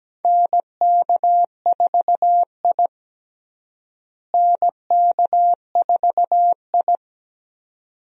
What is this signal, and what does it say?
Morse code, heard as a keyed tone, about 17 wpm, 700 Hz. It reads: NK4I NK4I
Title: USA Call Signs - 4 Characters - Repeated 2x - 17wpm